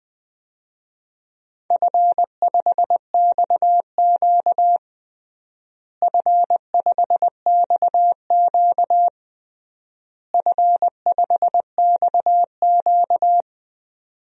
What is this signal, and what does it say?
Morse code, heard as a keyed tone, about 21 wpm, 700 Hz. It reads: F5XQ F5XQ F5XQ
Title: DXCC-100 Call Signs - 4 Characters - Repeated 3x - 20wpm